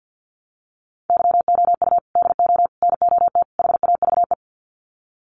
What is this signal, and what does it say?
Morse code, heard as a keyed tone, about 50 wpm, 700 Hz. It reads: YOU DO NOT HAVE
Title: Sets of 4 Words 50wpm